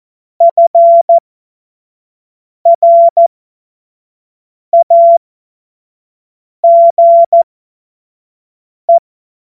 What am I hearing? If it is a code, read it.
Morse code, heard as a keyed tone, about 14 wpm, spaced as for 5 wpm, 680 Hz. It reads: FRAGE
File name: Morsekurs / Witze / Witz 10. Mai 2022 25 F70